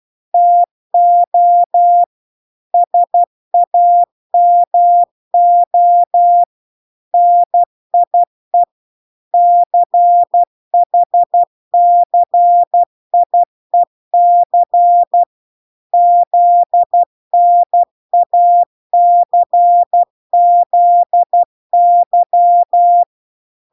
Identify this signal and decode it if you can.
Morse code, heard as a keyed tone, about 12 wpm, 700 Hz. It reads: TO SAMO NIE CHCIEC ZNACZY